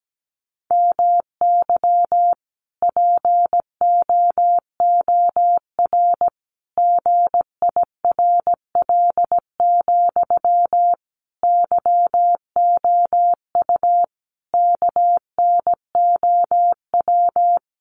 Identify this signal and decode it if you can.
Morse code, heard as a keyed tone, about 17 wpm, 700 Hz. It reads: MY POOR GIRL, YOU KNOW
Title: Sentences from Top 1000 Words 17wpm